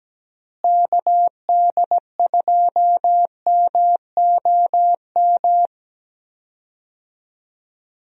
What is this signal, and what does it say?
Morse code, heard as a keyed tone, about 17 wpm, 700 Hz. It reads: KD2MOM
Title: USA Call Signs - 6 Characters - Repeated 3x - 17wpm